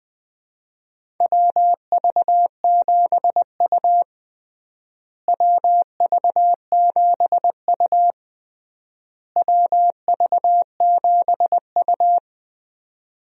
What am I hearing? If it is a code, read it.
Morse code, heard as a keyed tone, about 20 wpm, 700 Hz. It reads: WV7U WV7U WV7U
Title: USA Call Signs - 4 Characters - Repeated 3x - 20wpm